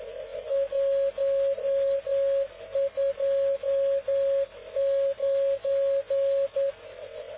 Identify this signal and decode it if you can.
Morse code, heard as a keyed tone, about 10 wpm, 550 Hz. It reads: 129
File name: cw / ubacw dk0wcy slow